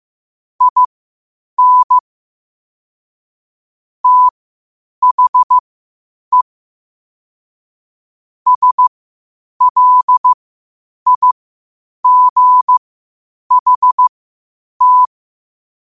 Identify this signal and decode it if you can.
Morse code, heard as a keyed tone, about 15 wpm, spaced as for 8 wpm, 1000 Hz. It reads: IN THE SLIGHT
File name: qotdfiles / 8wpm